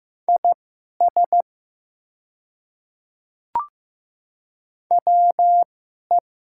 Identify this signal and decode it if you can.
Morse code, heard as a keyed tone, about 15 wpm, spaced as for 11 wpm, 700 Hz. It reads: IS WE
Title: 2x Character Spacing - Top 100 Words - 15wpm